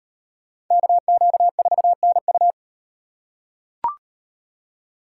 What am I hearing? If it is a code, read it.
Morse code, heard as a keyed tone, about 38 wpm, 700 Hz. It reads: KQ4NU